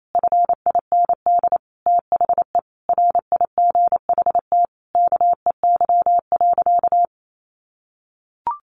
Morse code, heard as a keyed tone, about 28 wpm, 700 Hz. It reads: FIND THE RIGHT KEY.
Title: Sentences from Top 600 Words 28wpm